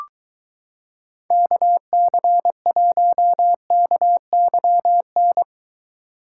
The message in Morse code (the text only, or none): KC1KYN